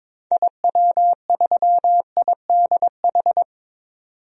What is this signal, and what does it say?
Morse code, heard as a keyed tone, about 22 wpm, 700 Hz. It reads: IW3IDH
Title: DXCC-100 Call Signs - 6 Characters - 22wpm